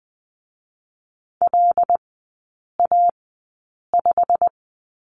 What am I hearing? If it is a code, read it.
Morse code, heard as a keyed tone, about 20 wpm, spaced as for 8 wpm, 700 Hz. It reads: LA5
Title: 087 Lesson 15e - Number 5 - Three Characters 20wpm